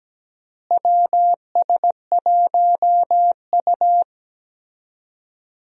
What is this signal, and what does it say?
Morse code, heard as a keyed tone, about 17 wpm, 700 Hz. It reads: WS1U
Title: USA Call Signs - 4 Characters - Repeated 2x - 17wpm